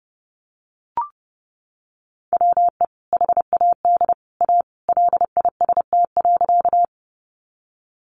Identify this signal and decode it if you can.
Morse code, heard as a keyed tone, about 30 wpm, 700 Hz. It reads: WE HAD A LIST.
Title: Sentences from Top 400 Words 30wpm